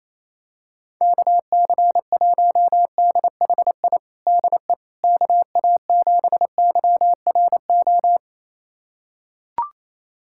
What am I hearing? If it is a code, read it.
Morse code, heard as a keyed tone, about 28 wpm, 700 Hz. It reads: KC1DHI DE KA7YRO
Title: USA Callsign DE Callsign - 6 Characters - 28wpm